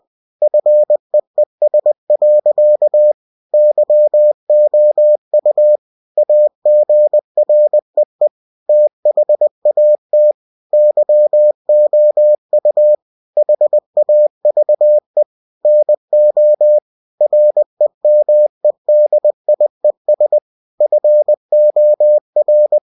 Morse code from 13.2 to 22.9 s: HAVE NO REMEDIES FOR